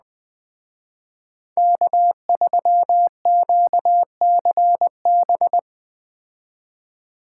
Morse code, heard as a keyed tone, about 21 wpm, 700 Hz. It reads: K3QCB
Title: Speed Racing - USA Call Signs 17wpm